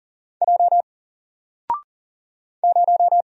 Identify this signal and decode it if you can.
Morse code, heard as a keyed tone, about 40 wpm, 700 Hz. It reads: J 0
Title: Single Letter-Number Alt 40wpm